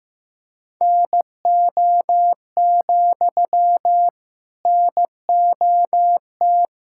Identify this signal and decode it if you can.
Morse code, heard as a keyed tone, about 15 wpm, 700 Hz. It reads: NO, NOT
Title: Sentences from Top 1000 Words - Review Part 6 - 15wpm